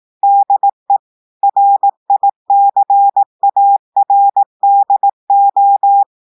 Morse code, heard as a keyed tone, about 18 wpm, 800 Hz. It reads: DE RICARDO
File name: audios / portugais / portugais texte en clair 1 vitesse 18 mots minutes